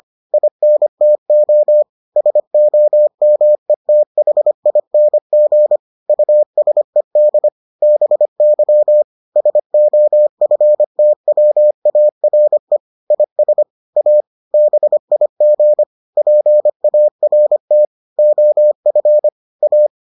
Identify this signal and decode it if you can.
Morse code, heard as a keyed tone, about 25 wpm, 600 Hz. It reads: INTO SOMETHING USED BY SOFTWARE IS A BIG PART OF A